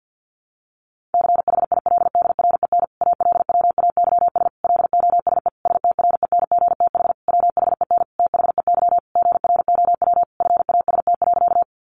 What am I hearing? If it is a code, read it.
Morse code, heard as a keyed tone, about 48 wpm, 700 Hz. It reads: CHILDREN ALWAYS LOSE STRENGTH WHEN THEY GROW FAST.
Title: Sentences from Top 1300 Words 50wpm